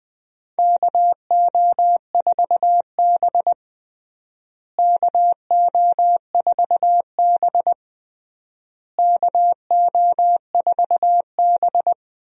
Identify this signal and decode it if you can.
Morse code, heard as a keyed tone, about 21 wpm, 700 Hz. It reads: KO4B KO4B KO4B
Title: USA Call Signs - 4 Characters - Repeated 3x - 20wpm